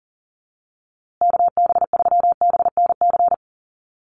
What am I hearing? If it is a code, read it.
Morse code, heard as a keyed tone, about 40 wpm, 700 Hz. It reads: KB3BNC
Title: USA Call Signs Repeated 3x 40wpm